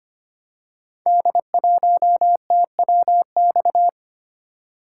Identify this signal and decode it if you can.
Morse code, heard as a keyed tone, about 25 wpm, 700 Hz. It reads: D1TWX
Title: Code Groups - 5 Letter:Numbers - 25wpm